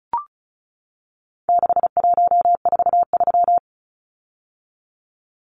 Morse code, heard as a keyed tone, about 34 wpm, 700 Hz. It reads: 6143
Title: Numbers - 4-Digit 35wpm